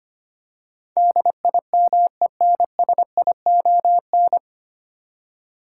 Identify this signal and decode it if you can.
Morse code, heard as a keyed tone, about 25 wpm, 700 Hz. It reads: DIMENSION
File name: Nine Letter Words - 25wpm